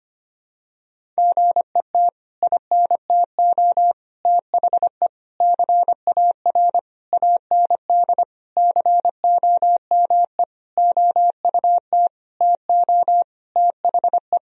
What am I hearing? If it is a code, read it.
Morse code, heard as a keyed tone, about 25 wpm, 700 Hz. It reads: GET INTO THE CAR AND COME OUT TO THE
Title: Sentences from Top 1000 Words - Review Part 1 - 25wpm